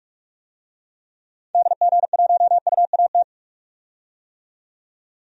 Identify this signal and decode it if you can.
Morse code, heard as a keyed tone, about 43 wpm, 700 Hz. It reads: DG1UAT
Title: DXCC-100 Call Signs - 6 Characters - 45wpm